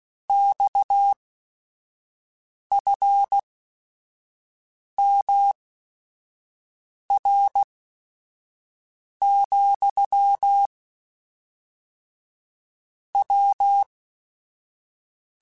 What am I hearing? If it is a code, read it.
Morse code, heard as a keyed tone, about 16 wpm, spaced as for 5 wpm, 800 Hz. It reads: XFMR, W
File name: Technical Subjects / Technical Subjects-mp3s / Sw Pwr Sup @5